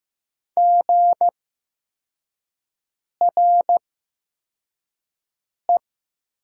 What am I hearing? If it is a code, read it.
Morse code, heard as a keyed tone, about 15 wpm, spaced as for 4 wpm, 700 Hz. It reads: GRE
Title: 8x Character Spacing - Top 300 Words - 15wpm